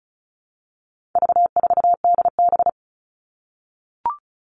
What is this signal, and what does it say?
Morse code, heard as a keyed tone, about 34 wpm, 700 Hz. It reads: V4DB
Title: All Cut Numbers - 4-Digit 35wpm